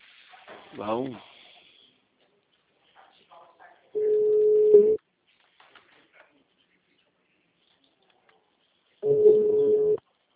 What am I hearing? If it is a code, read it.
Ringback tone (Europe, 425 Hz), with speech.
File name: out- -2001- - - .33895